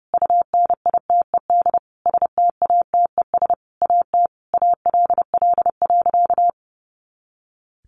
Morse code, heard as a keyed tone, about 30 wpm, 700 Hz. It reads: UNITED STATES AT ALL.
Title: A Sound Of Thunder 30wpm